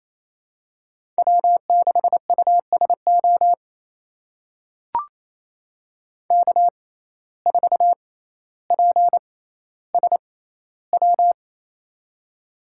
Morse code, heard as a keyed tone, about 28 wpm, spaced as for 14 wpm, 700 Hz. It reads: W6USO K4PSW
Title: USA Call Signs - 5 Characters - 6x Character Spacing - 28wpm